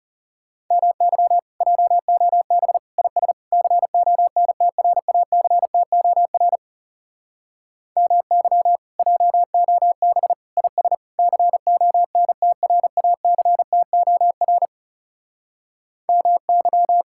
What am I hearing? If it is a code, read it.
Morse code, heard as a keyed tone, about 40 wpm, 700 Hz. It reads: MY JOB IS CONTRACTOR MY JOB IS CONTRACTOR MY